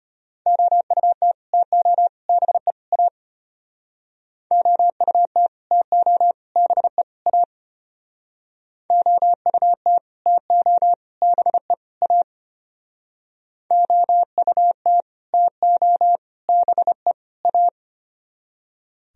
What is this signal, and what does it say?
Morse code, heard as a keyed tone, about 31 wpm, 700 Hz. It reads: OUT TO BE A OUT TO BE A OUT TO BE A OUT TO BE A
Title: Speed-Racing - Sets of 4 Words 25wpm